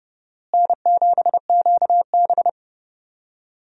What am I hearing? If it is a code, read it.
Morse code, heard as a keyed tone, about 30 wpm, 700 Hz. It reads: N7QB
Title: USA Call Signs - 4 Characters - Repeated 3x - 30wpm